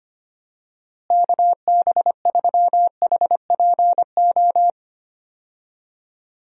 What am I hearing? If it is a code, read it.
Morse code, heard as a keyed tone, about 25 wpm, 700 Hz. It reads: KB3HPO
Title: USA Call Signs - 6 Characters - Repeated 3x - 25wpm